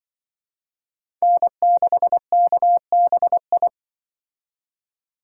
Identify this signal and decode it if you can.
Morse code, heard as a keyed tone, about 24 wpm, 700 Hz. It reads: N6KBI
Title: Speed Racing - USA 1x3 Call Signs 20wpm